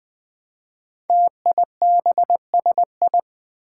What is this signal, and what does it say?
Morse code, heard as a keyed tone, about 20 wpm, 700 Hz. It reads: TIBSI
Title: Five Challenging Letter-Number Combinations 20wpm